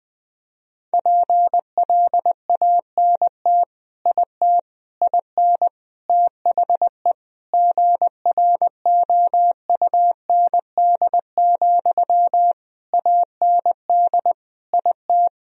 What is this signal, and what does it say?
Morse code, heard as a keyed tone, about 20 wpm, 700 Hz. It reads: PLANT IT IN THE GROUND, AND IT
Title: Sentences from Top 1000 Words - Review Part 2 - 20wpm